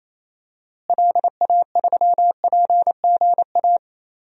Morse code, heard as a keyed tone, about 28 wpm, 700 Hz. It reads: LA3PGA
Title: DXCC-100 Call Signs - 6 Characters - Repeated 3x - 28wpm